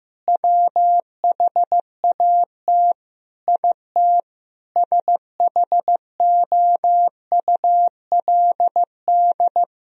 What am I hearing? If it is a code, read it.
Morse code, heard as a keyed tone, about 15 wpm, 700 Hz. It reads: WHAT IT SHOULD
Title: Sentences from Top 1000 Words - Review Part 4 - 15wpm